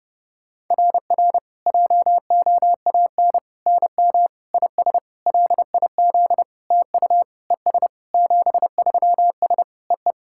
Morse code, heard as a keyed tone, about 30 wpm, 690 Hz. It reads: RR JOAN NM IS LIZ TU ES 73S EE
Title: SOTA QSO Practice 30wpm